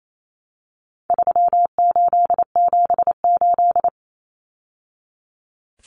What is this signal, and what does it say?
Morse code, heard as a keyed tone, about 28 wpm, 700 Hz. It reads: 3878
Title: Numbers - 4-Digit 28wpm